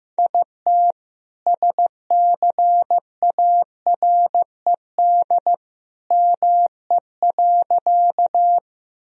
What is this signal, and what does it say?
Morse code, heard as a keyed tone, about 15 wpm, 700 Hz. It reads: IT SCARED ME.
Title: Sentences from Top 1000 Words - Review Part 4 - 15wpm